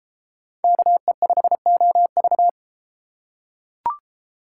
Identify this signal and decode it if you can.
Morse code, heard as a keyed tone, about 32 wpm, 700 Hz. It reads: KE5OV